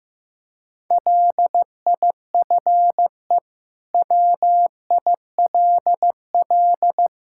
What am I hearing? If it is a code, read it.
Morse code, heard as a keyed tone, about 15 wpm, 700 Hz. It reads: LIFE WILL
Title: Sentences from Top 1000 Words - Review Part 5 - 15wpm